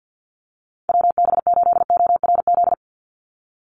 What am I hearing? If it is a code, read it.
Morse code, heard as a keyed tone, about 50 wpm, 700 Hz. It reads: WB8ORZ
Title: USA Call Signs 50wpm